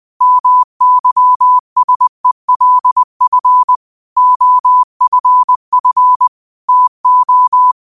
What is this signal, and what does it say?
Morse code, heard as a keyed tone, about 20 wpm, 1000 Hz. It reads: MYSELF OFF TO
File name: qotdfiles / 20wpm